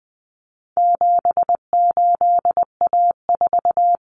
Morse code, heard as a keyed tone, about 20 wpm, 700 Hz. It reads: 78A4
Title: Common Cut Numbers - 4-Digit 20wpm